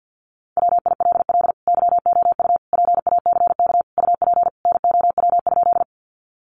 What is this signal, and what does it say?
Morse code, heard as a keyed tone, about 50 wpm, 700 Hz. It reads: WILL YOU PACK UP NOW?
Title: Sentences from Top 1300 Words 50wpm